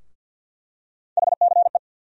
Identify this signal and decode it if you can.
Morse code, heard as a keyed tone, about 49 wpm, 700 Hz. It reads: HCE
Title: USA Call Sign Postfixes - 3 Letters 50wpm